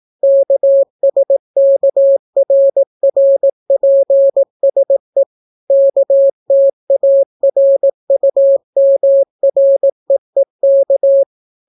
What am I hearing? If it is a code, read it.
Morse code, heard as a keyed tone, about 18 wpm, 550 Hz. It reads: KSKRRPSE KTARUMREEK